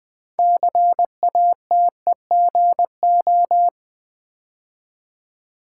Morse code, heard as a keyed tone, about 20 wpm, 700 Hz. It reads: CATEGO
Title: Build A Word - 3 Letters At A Time - 8-Letter Words - 20wpm